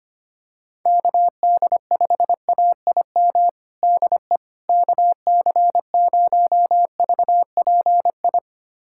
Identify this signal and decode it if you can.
Morse code, heard as a keyed tone, about 25 wpm, 700 Hz. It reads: KD5AIM DE KC0VPI
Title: USA Callsign DE Callsign 25wpm